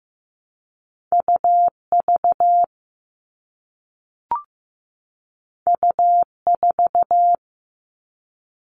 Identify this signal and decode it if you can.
Morse code, heard as a keyed tone, about 15 wpm, 700 Hz. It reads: UV U4
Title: All Cut Numbers - 2-Digit 15wpm